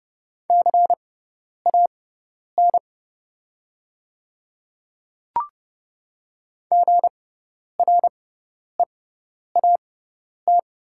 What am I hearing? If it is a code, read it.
Morse code, heard as a keyed tone, about 30 wpm, spaced as for 10 wpm, 700 Hz. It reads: CAN GREAT
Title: 6x Character Spacing - Top 200 Words - 30wpm